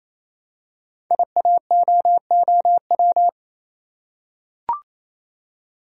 Morse code, heard as a keyed tone, about 27 wpm, 700 Hz. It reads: IAOOW
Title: Code Groups - 5 Letters - 28wpm